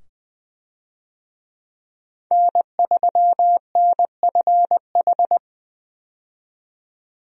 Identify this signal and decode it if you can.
Morse code, heard as a keyed tone, about 20 wpm, 700 Hz. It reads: N3NFH
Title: USA Call Signs - 5 Characters - 20wpm